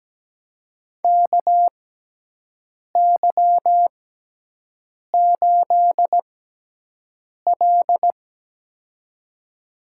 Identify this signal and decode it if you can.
Morse code, heard as a keyed tone, about 17 wpm, spaced as for 6 wpm, 700 Hz. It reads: KY8L
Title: USA Call Signs - 4 Characters - 6x Character Spacing - 17wpm